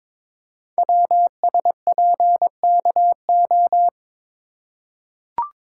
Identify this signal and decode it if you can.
Morse code, heard as a keyed tone, about 22 wpm, 700 Hz. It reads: WSPKO
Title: Code Groups - 5 Letters - 22wpm